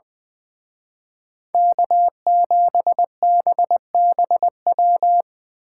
Morse code, heard as a keyed tone, about 23 wpm, 700 Hz. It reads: K7BBW
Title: Speed Racing - USA 1x3 Call Signs 20wpm